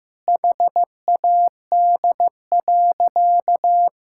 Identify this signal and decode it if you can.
Morse code, heard as a keyed tone, about 15 wpm, 700 Hz. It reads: HAD.
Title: Sentences from Top 1000 Words - Review Part 6 - 15wpm